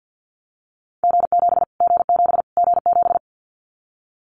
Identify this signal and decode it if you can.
Morse code, heard as a keyed tone, about 50 wpm, 700 Hz. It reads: G7 G7 G7
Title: Callsign Prefixes for Top 25 DXCC - Repeated 3x - 50wpm